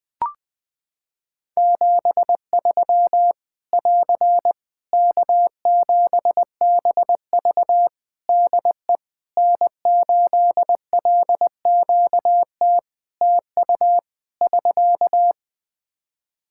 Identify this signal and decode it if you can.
Morse code, heard as a keyed tone, about 20 wpm, 700 Hz. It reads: 73 <AR> K7BV DE N8LQT TU <SK>
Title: Practice QSO - Exchange 4 - Ending - 20wpm